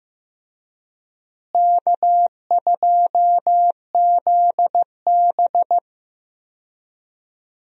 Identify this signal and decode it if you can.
Morse code, heard as a keyed tone, about 15 wpm, 700 Hz. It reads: K2ZB
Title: USA Call Signs - 4 Characters - Repeated 3x - 15wpm